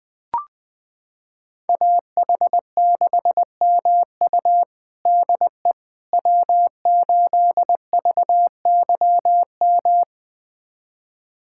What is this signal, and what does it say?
Morse code, heard as a keyed tone, about 20 wpm, 700 Hz. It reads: AH6MU DE W8VYM
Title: USA Callsign DE Callsign - 5 Characters - 20wpm